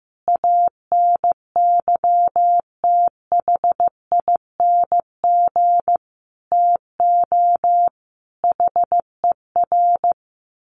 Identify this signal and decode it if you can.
Morse code, heard as a keyed tone, about 14 wpm, 700 Hz. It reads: ANYTHING TO HER
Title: Sentences from Top 1000 Words 15wpm